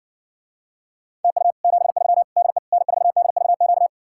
Morse code, heard as a keyed tone, about 68 wpm, 700 Hz. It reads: TU 73 DE N4DVQ